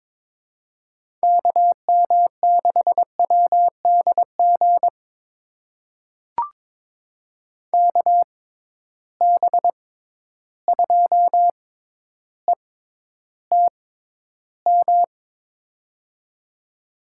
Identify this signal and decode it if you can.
Morse code, heard as a keyed tone, about 21 wpm, spaced as for 11 wpm, 700 Hz. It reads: KM6WDG KB2ETM
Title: USA Call Signs - 6 Characters - 6x Character Spacing - 22wpm